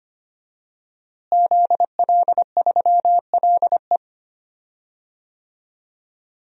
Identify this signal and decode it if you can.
Morse code, heard as a keyed tone, about 25 wpm, 700 Hz. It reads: ZL3LE
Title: DXCC-100 Call Signs - 5 Characters - Repeated 2x - 25wpm